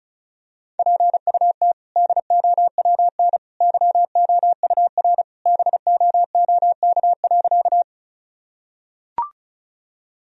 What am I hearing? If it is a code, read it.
Morse code, heard as a keyed tone, about 36 wpm, 700 Hz. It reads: PUT DOWN YOUR BOOK.
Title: Sentences from Top 1000 Words - Review Part 1 - 35wpm